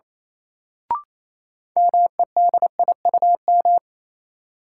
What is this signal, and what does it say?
Morse code, heard as a keyed tone, about 28 wpm, 700 Hz. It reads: MEDIUM
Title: Sentences from Top 900 Words - New Word Drill - 28wpm